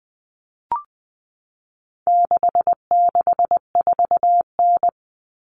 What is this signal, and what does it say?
Morse code, heard as a keyed tone, about 20 wpm, 700 Hz. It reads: 664N
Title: Common Cut Numbers - 4-Digit 20wpm